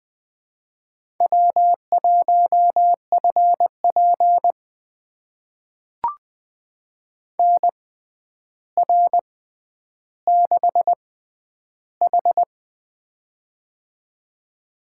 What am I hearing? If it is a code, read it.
Morse code, heard as a keyed tone, about 20 wpm, spaced as for 10 wpm, 700 Hz. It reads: W1FP NR6H
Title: USA Call Signs - 4 Characters - 6x Character Spacing - 20wpm